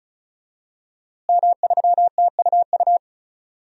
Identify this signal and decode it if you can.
Morse code, heard as a keyed tone, about 36 wpm, 700 Hz. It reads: M3TUU
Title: DXCC-100 Call Signs - 5 Characters - Repeated 2x - 35wpm